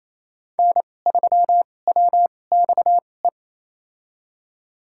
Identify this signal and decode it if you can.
Morse code, heard as a keyed tone, about 28 wpm, spaced as for 20 wpm, 700 Hz. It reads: N3WXE
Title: USA Call Signs - 2x Character Spacing - 28wpm